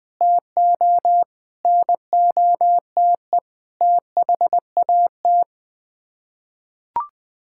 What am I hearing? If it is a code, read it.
Morse code, heard as a keyed tone, about 21 wpm, 700 Hz. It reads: TO NOTE THAT
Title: Sets of 6 Words - 20wpm